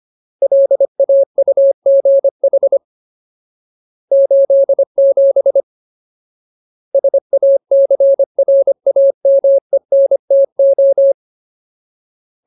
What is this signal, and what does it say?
Morse code, heard as a keyed tone, about 25 wpm, 550 Hz. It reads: LAUGH 87 SACRAMENTO
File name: Ultimate Word List / 25WPM Ultimate